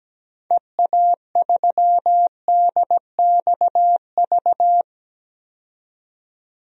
Morse code, heard as a keyed tone, about 17 wpm, 700 Hz. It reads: EA3DXV
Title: DXCC-100 Call Signs - 6 Characters - 17wpm